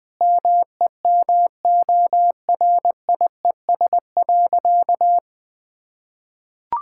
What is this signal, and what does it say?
Morse code, heard as a keyed tone, about 21 wpm, 700 Hz. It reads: MEMORIES.
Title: Difficult Sentences with Hints - 20wpm